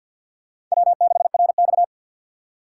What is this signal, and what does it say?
Morse code, heard as a keyed tone, about 51 wpm, 700 Hz. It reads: W6RX
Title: USA Call Signs - 3x Character Spacing - 50wpm